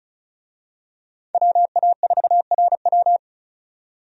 Morse code, heard as a keyed tone, about 36 wpm, 700 Hz. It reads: WA4RW
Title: USA Call Signs - 5 Characters - 35wpm